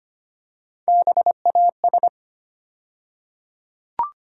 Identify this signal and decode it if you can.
Morse code, heard as a keyed tone, about 25 wpm, 700 Hz. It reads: BAS
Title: USA Call Sign Postfixes - 3 Letters 25wpm